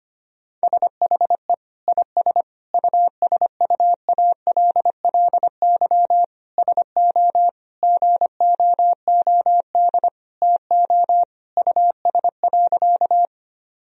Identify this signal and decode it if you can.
Morse code, heard as a keyed tone, about 25 wpm, 700 Hz. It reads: SHE IS USUALLY SO GOOD TO US.
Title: Sentences from Top 1000 Words - Review Part 3 - 25wpm